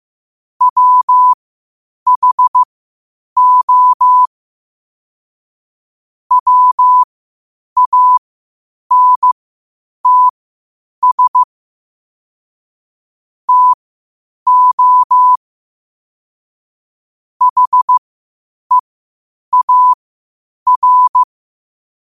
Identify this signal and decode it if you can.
Morse code, heard as a keyed tone, about 15 wpm, spaced as for 8 wpm, 1000 Hz. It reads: WHO WANTS TO HEAR
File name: qotdfiles / 8wpm